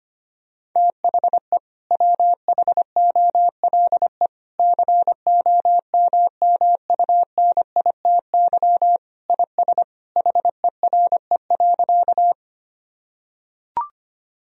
Morse code, heard as a keyed tone, about 25 wpm, 700 Hz. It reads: THE WHOLE COMMUNITY IS HERE.
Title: Sentences from Top 1000 Words - Review Part 4 - 25wpm